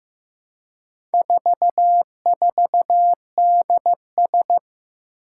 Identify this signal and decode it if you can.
Morse code, heard as a keyed tone, about 15 wpm, 700 Hz. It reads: 44DS
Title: Four Challenging Letter-Number Combinations 15wpm